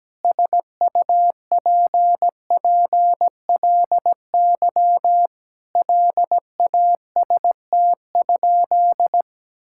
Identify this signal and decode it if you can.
Morse code, heard as a keyed tone, about 17 wpm, 700 Hz. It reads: SUPPLY LAST?
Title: Sentences from Top 1000 Words - Review Part 5 - 17wpm